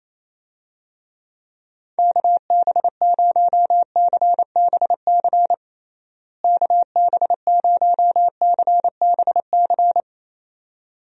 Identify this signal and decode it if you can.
Morse code, heard as a keyed tone, about 28 wpm, 700 Hz. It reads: KB0CBC KB0CBC
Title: USA Call Signs - 6 Characters - Repeated 2x - 28wpm